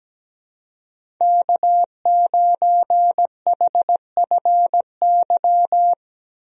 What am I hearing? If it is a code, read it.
Morse code, heard as a keyed tone, about 17 wpm, 700 Hz. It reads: K9HFY
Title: USA Call Signs - 5 Characters - Repeated 2x - 17wpm